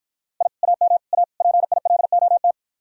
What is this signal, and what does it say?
Morse code, heard as a keyed tone, about 51 wpm, 700 Hz. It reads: I AM A PILOT